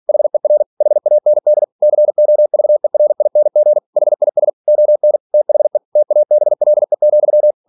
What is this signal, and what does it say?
Morse code, heard as a keyed tone, about 46 wpm, 600 Hz. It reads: HER HAND COVERING HIS ON THE TABLE,
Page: Never 47wpm